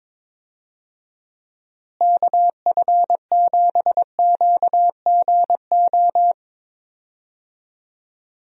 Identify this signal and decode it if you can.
Morse code, heard as a keyed tone, about 22 wpm, 700 Hz. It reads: KF7QGO
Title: USA Call Signs - 6 Characters - Repeated 2x - 22wpm